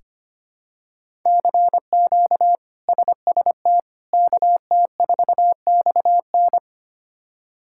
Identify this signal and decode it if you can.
Morse code, heard as a keyed tone, about 25 wpm, 700 Hz. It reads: CQ SST KT4XN
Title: K1USN SST QSO Practice - 25wpm